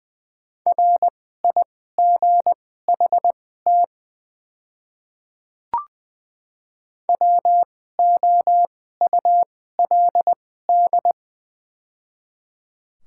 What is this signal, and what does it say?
Morse code, heard as a keyed tone, about 20 wpm, spaced as for 14 wpm, 700 Hz. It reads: RIGHT WOULD
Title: 2x Character Spacing - Top 300 Words - 20wpm